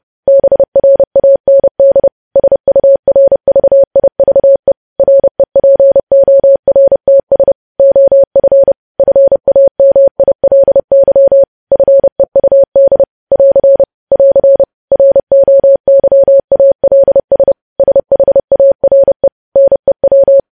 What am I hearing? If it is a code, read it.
Morse code, heard as a keyed tone, about 30 wpm, 550 Hz. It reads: BRAND SURVIVE REPORTS OF FAMILY FEUD <AR> <AR> ROYALS SHARE NEW
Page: news headlines in morse code at 30 wpm